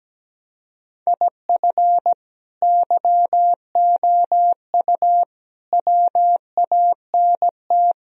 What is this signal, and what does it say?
Morse code, heard as a keyed tone, about 18 wpm, 700 Hz. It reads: IF YOU WANT
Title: Speed-Racing - Sets of 4 Words 17wpm